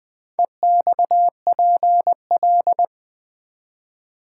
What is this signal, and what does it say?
Morse code, heard as a keyed tone, about 20 wpm, 700 Hz. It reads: EXPL
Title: Build A Word - 4 Letters At A Time - 12-Letter Words - 20wpm